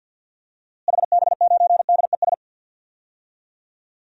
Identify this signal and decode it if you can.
Morse code, heard as a keyed tone, about 48 wpm, 700 Hz. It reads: HB9DES
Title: DXCC-100 Call Signs - 6 Characters - Repeated 2x - 50wpm